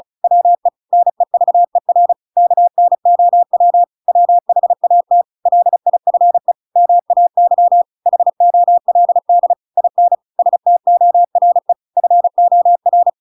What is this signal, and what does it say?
Morse code, heard as a keyed tone, about 34 wpm, 700 Hz. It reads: WE NEVER KNOW WHAT LIFE MAY HOLD IN STORE FOR
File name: Pinocchio / Pinocchio ch7 35 35 ik1hge